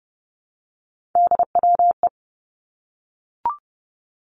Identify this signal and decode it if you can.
Morse code, heard as a keyed tone, about 30 wpm, 700 Hz. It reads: DWE
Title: Three Challenging Letter-Number Combinations 30wpm